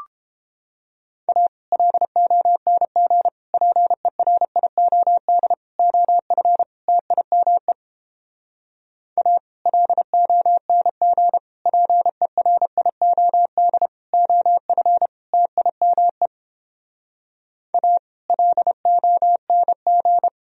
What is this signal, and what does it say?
Morse code, heard as a keyed tone, about 29 wpm, 700 Hz. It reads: A LONG PERIOD OF TIME A LONG PERIOD OF TIME A LONG